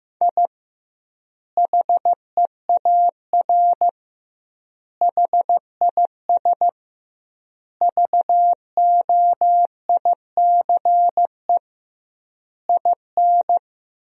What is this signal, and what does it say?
Morse code, heard as a keyed tone, about 16 wpm, 700 Hz. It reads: I HEAR HIS VOICE IN
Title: 2x Word-Spacing - Sentences from Top 500 Words - 15wpm